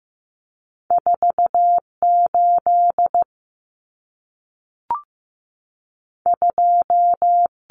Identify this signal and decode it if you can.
Morse code, heard as a keyed tone, about 15 wpm, 700 Hz. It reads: 48 2
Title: Common Cut Numbers - 2-Digit 15wpm